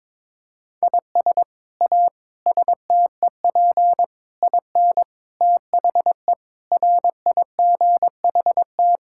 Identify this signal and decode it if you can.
Morse code, heard as a keyed tone, about 22 wpm, 700 Hz. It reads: IS A STEP IN THE RIGHT
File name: practice / 22wpm / Sets of 8 Words - 22wpm